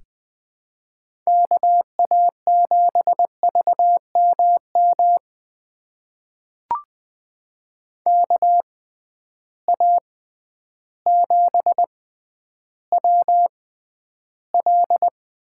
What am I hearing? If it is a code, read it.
Morse code, heard as a keyed tone, about 20 wpm, spaced as for 11 wpm, 700 Hz. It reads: KA7VMM KA7WL
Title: USA Call Signs - 6x Character Spacing - 20wpm